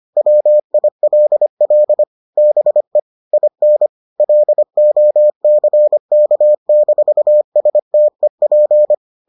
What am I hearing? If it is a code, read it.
Morse code, heard as a keyed tone, about 25 wpm, 600 Hz. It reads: WILL BE IN LOCK-STEP